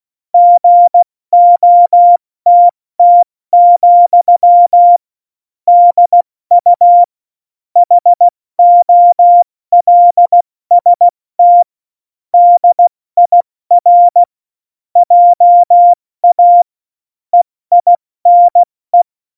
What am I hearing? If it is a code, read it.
Morse code, heard as a keyed tone, about 16 wpm, 700 Hz. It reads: GOTT, DU HOLST DIR JA EINE
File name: Morsekurs / Witze / Witz 12. Dezember 2017, 70 F80